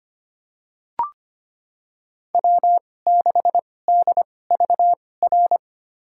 Morse code, heard as a keyed tone, about 25 wpm, spaced as for 18 wpm, 700 Hz. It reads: W6DVR
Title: USA Call Signs - 2x Character Spacing - 25wpm